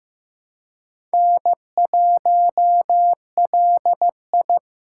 Morse code, heard as a keyed tone, about 15 wpm, 700 Hz. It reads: N1LI
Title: USA Call Signs - 4x Character Spacing - 15wpm